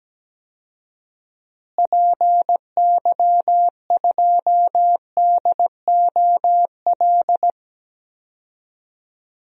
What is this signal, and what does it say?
Morse code, heard as a keyed tone, about 17 wpm, 700 Hz. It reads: PY2DOL
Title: DXCC-100 Call Signs - 6 Characters - Repeated 2x - 17wpm